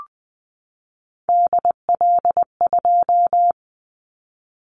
Morse code, heard as a keyed tone, about 20 wpm, 700 Hz. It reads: DL2
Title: Callsign Prefixes for Top 50 DXCC - 20wpm